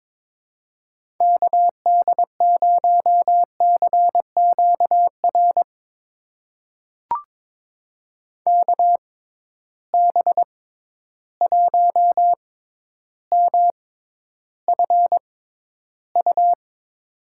Morse code, heard as a keyed tone, about 21 wpm, spaced as for 11 wpm, 700 Hz. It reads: KD0CQR KB1MFU
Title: USA Call Signs - 6 Characters - 6x Character Spacing - 22wpm